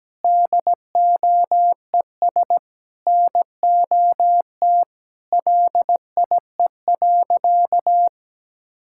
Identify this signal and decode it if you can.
Morse code, heard as a keyed tone, about 17 wpm, 700 Hz. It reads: DOES NOT LIE.
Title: Sentences from Top 1600 Words 17wpm